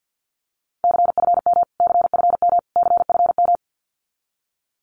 Morse code, heard as a keyed tone, about 50 wpm, 700 Hz. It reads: CFM CFM CFM
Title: Abbreviations Repeated 3x - 50wpm